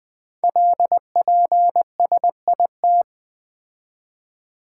Morse code, heard as a keyed tone, about 20 wpm, 700 Hz. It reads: LPSIT
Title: Code Groups - 5 Letters - 20wpm